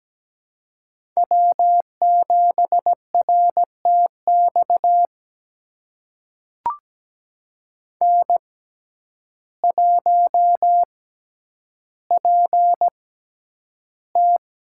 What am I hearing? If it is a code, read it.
Morse code, heard as a keyed tone, about 17 wpm, spaced as for 9 wpm, 700 Hz. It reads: W7RTX N1PT
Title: USA Call Signs - 6x Character Spacing - 17wpm